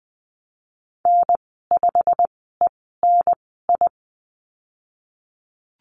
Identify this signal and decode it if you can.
Morse code, heard as a keyed tone, about 20 wpm, spaced as for 14 wpm, 700 Hz. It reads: N5ENI
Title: 147 Lesson 24g - Letter y - Call Signs 20wpm